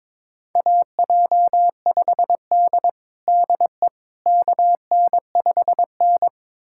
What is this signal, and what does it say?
Morse code, heard as a keyed tone, about 22 wpm, 700 Hz. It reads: AJ5D DE KN5N
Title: USA Callsign DE Callsign - 4 Characters - 22wpm